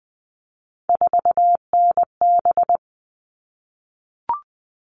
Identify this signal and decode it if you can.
Morse code, heard as a keyed tone, about 20 wpm, 700 Hz. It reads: 4NB
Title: All Cut Numbers - 3-Digit 20wpm